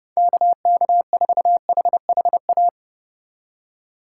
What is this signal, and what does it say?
Morse code, heard as a keyed tone, about 29 wpm, 700 Hz. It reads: KK4HHA